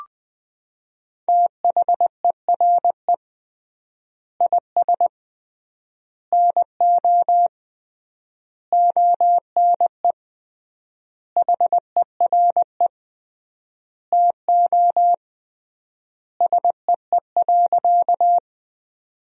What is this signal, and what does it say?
Morse code, heard as a keyed tone, about 20 wpm, 700 Hz. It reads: THERE IS NO ONE HERE TO SEE.
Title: 3x Word-Spacing - Sentences from Top 200 Words - 20wpm